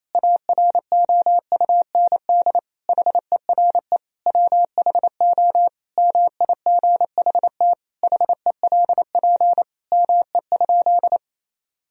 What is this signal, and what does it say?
Morse code, heard as a keyed tone, about 28 wpm, 700 Hz. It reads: AROUND HERE WHO MIGHT HELP ME?
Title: Sentences from Top 1000 Words - Review Part 3 - 28wpm